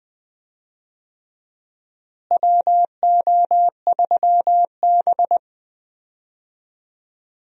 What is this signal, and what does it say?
Morse code, heard as a keyed tone, about 21 wpm, 700 Hz. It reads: WO3B
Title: USA Call Signs - 4 Characters - Repeated 3x - 20wpm